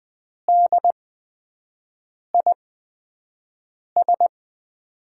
Morse code, heard as a keyed tone, about 20 wpm, spaced as for 5 wpm, 700 Hz. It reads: DIS
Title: USA Call Signs - 5 Characters - 8x Character Spacing - 20wpm